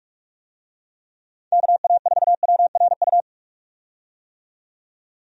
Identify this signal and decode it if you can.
Morse code, heard as a keyed tone, about 46 wpm, 700 Hz. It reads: KA4WRU
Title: USA Call Signs - 6 Characters - 45wpm